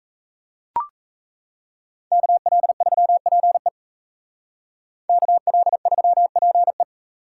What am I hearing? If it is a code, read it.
Morse code, heard as a keyed tone, about 36 wpm, 700 Hz. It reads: KL3PE KL3PE